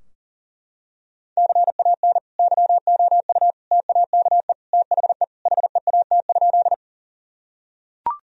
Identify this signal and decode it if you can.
Morse code, heard as a keyed tone, about 40 wpm, 700 Hz. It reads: CAN YOU TAKE THE HEAT?
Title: Sentences from Top 1000 Words - Review Part 3 - 40wpm